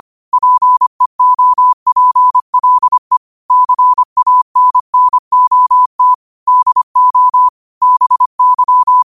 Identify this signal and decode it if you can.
Morse code, heard as a keyed tone, about 25 wpm, 1000 Hz. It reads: PEOPLE CANNOT DO BY
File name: qotdfiles / 25wpm